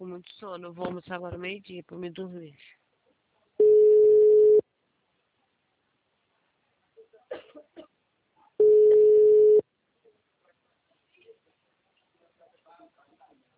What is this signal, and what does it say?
Ringback tone (Europe, 425 Hz), with speech.